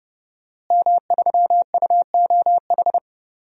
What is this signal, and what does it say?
Morse code, heard as a keyed tone, about 30 wpm, 700 Hz. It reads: M3UOH
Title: DXCC-100 Call Signs - 5 Characters - Repeated 2x - 30wpm